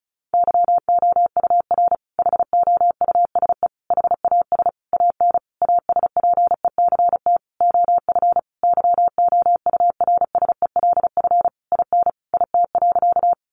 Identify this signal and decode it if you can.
Morse code, heard as a keyed tone, about 34 wpm, 700 Hz. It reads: YOUR HOUSE HAS AN ASPECT OF YOURSELF IN IT.
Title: Sentences from Top 1300 Words 35wpm